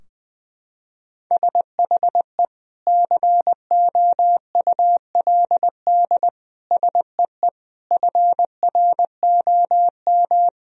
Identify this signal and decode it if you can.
Morse code, heard as a keyed tone, about 20 wpm, 700 Hz. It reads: SHE COULD SEE FROM
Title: Sentences from Top 1000 Words - Review Part 2 - 20wpm